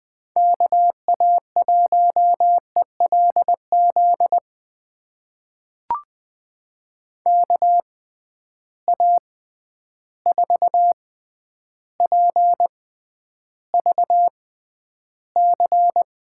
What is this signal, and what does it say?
Morse code, heard as a keyed tone, about 20 wpm, spaced as for 10 wpm, 700 Hz. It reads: KA1ELZ KA4PVC
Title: USA Call Signs - 6 Characters - 6x Character Spacing - 20wpm